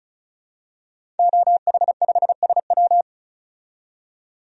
Morse code, heard as a keyed tone, about 36 wpm, 700 Hz. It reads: OH5SW
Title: DXCC-100 Call Signs - 5 Characters - 35wpm